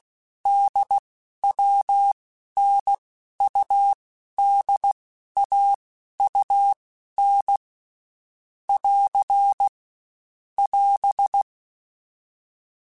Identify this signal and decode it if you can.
Morse code, heard as a keyed tone, about 16 wpm, 800 Hz. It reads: D W N U D A U N <AR> <AS>
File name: Basic Letter Groups By7Groups / AWUNDG / AWUNDG@15